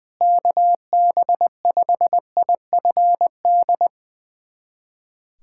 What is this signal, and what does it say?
Morse code, heard as a keyed tone, about 21 wpm, 700 Hz. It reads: KB5IFD
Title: USA Callsign DE Callsign 20wpm